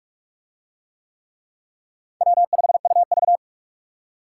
Morse code, heard as a keyed tone, about 38 wpm, 700 Hz. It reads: W5UV